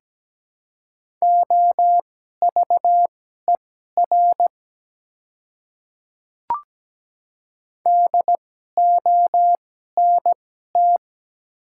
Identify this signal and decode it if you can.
Morse code, heard as a keyed tone, about 17 wpm, spaced as for 12 wpm, 700 Hz. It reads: OVER DONT
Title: 2x Character Spacing - Top 300 Words - 17wpm